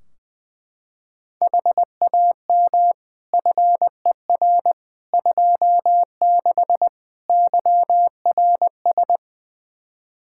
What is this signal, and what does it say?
Morse code, heard as a keyed tone, about 20 wpm, 700 Hz. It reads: HAM FER 26 YRS
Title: Practice QSO - Exchange 3 - More Info - 20wpm